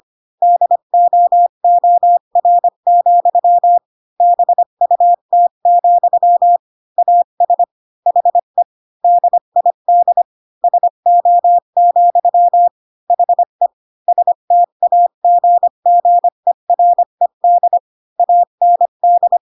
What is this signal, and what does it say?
Morse code, heard as a keyed tone, about 25 wpm, 700 Hz. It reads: DOOR, BUT, AS HE DID SO, HE STAGGERED AND
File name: Pinocchio / Pinocchio ch7 20 25 ik1hge